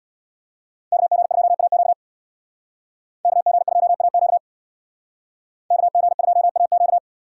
Morse code, heard as a keyed tone, about 63 wpm, 700 Hz. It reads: KG2AQ KG2AQ KG2AQ